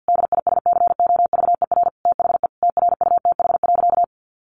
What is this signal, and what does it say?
Morse code, heard as a keyed tone, about 49 wpm, 700 Hz. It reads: DISCOVER THE TRUTH.
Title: Sentences from Top 1100 Words 50wpm